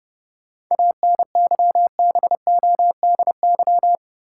Morse code, heard as a keyed tone, about 31 wpm, 700 Hz. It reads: ANYBODY
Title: Sets of 5 Words - New Word Drill - 30wpm